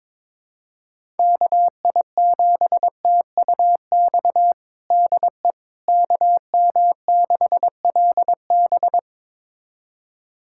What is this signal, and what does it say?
Morse code, heard as a keyed tone, about 21 wpm, 700 Hz. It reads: KI7TUX DE KM6LB
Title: USA Callsign DE Callsign 22wpm